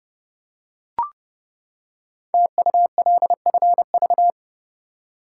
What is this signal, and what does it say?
Morse code, heard as a keyed tone, about 30 wpm, 690 Hz. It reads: TULFV
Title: Code Groups - 5 Letters - 30wpm